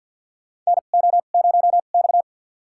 Morse code, heard as a keyed tone, about 49 wpm, spaced as for 35 wpm, 700 Hz. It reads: NO0X
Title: USA Call Signs - 4 Characters - 2x Character Spacing - 50wpm